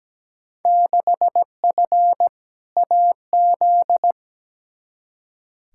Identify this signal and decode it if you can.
Morse code, heard as a keyed tone, about 17 wpm, 700 Hz. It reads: 6F AZ
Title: Field Day Exchanges Encore - 17wpm